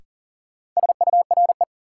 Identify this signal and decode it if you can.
Morse code, heard as a keyed tone, about 40 wpm, 700 Hz. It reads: SURE
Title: Sets of 3 Words - New Word Drill - 40wpm